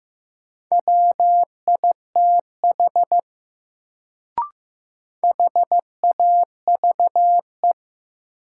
Morse code, heard as a keyed tone, about 15 wpm, 700 Hz. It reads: WITH HAVE
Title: Sets of 2 Words - New Word Drill - 15wpm